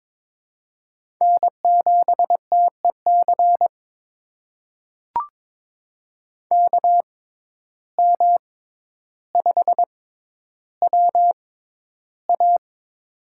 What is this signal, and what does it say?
Morse code, heard as a keyed tone, about 22 wpm, spaced as for 11 wpm, 700 Hz. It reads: N7TEC KM5WA
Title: USA Call Signs - 5 Characters - 6x Character Spacing - 22wpm